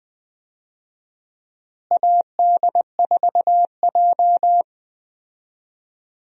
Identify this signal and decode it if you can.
Morse code, heard as a keyed tone, about 20 wpm, 700 Hz. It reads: AD4J
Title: USA Call Signs - 4 Characters - Repeated 2x - 20wpm